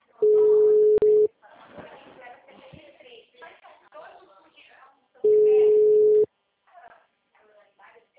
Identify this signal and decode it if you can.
Ringback tone (Europe, 425 Hz), with speech.
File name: out- -2026- - - .54942